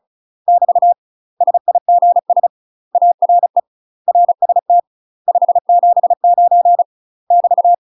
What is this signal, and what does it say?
Morse code, heard as a keyed tone, about 34 wpm, 700 Hz. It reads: <BT> SIGS ARE RST 579 <BT>
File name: UK MASTER QSOs TRAINING / 35WPM / UK MASTER QSOs 0034